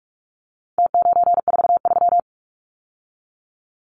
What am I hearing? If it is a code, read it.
Morse code, heard as a keyed tone, about 44 wpm, 700 Hz. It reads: T943
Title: Numbers - Contest Style - 4-Digit 45wpm